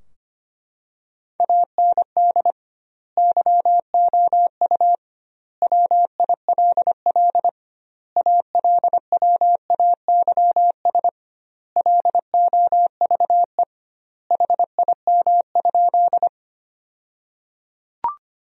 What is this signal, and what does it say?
Morse code, heard as a keyed tone, about 25 wpm, 700 Hz. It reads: AND YOU WILL ALWAYS LOVE HIM?
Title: 2x Word-Spacing - Sentences from Top 400 Words - 25wpm